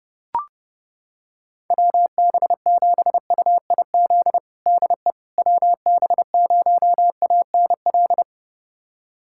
Morse code, heard as a keyed tone, about 32 wpm, 700 Hz. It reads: WB7UIZ DE WB0ANL
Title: USA Callsign DE Callsign - 6 Characters - 30wpm